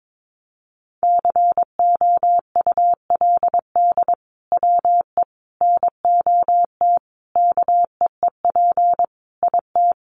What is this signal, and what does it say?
Morse code, heard as a keyed tone, about 22 wpm, 700 Hz. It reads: COULD WE NOT KEEP IT
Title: Sentences from Top 1200 Words 22wpm